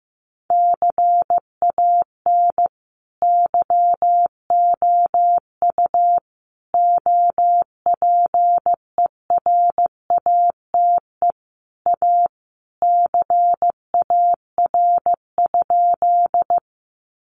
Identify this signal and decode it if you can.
Morse code, heard as a keyed tone, about 15 wpm, 700 Hz. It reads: CAN YOU OPERATE A CAR?
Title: Sentences from Top 1500 Words 15wpm